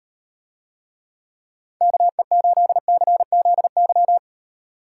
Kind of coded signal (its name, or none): Morse code